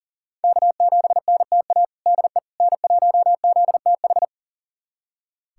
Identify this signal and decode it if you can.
Morse code, heard as a keyed tone, about 40 wpm, 700 Hz. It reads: K7NTA DE N1ZTH
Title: USA Callsign DE Callsign - 5 Characters - 40wpm